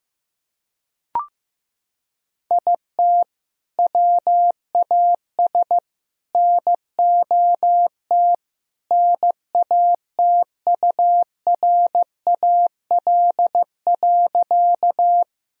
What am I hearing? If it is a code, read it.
Morse code, heard as a keyed tone, about 15 wpm, 700 Hz. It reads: IT WAS NOT NATURAL.
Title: Sentences from Top 1000 Words - Review Part 4 - 15wpm